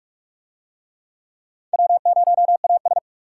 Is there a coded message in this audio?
Yes: Morse code from 1.7 to 3.0 s, heard as a keyed tone, about 44 wpm, 700 Hz. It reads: W0AS